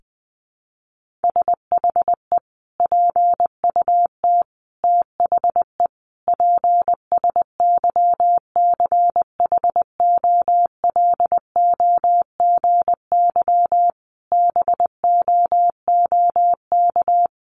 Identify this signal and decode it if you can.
Morse code, heard as a keyed tone, about 19 wpm, 700 Hz. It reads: SHE PUT THE PSYCHOLOGY BOOK
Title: Sentences from Top 1400 Words 20wpm